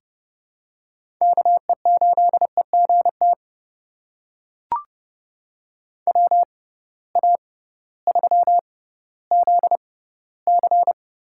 Morse code, heard as a keyed tone, about 31 wpm, spaced as for 16 wpm, 700 Hz. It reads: KE8EGT WA3ZC
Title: USA Call Signs - 6 Characters - 6x Character Spacing - 30wpm